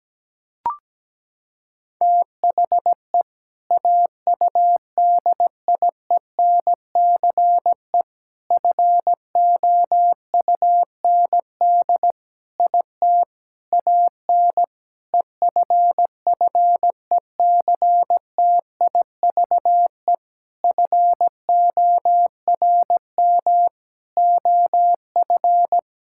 Morse code, heard as a keyed tone, about 17 wpm, 700 Hz. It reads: THE AUDIENCE FOUND IT AN EFFECTIVE FORM OF
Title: Sentences from Top 1000 Words - Review Part 6 - 17wpm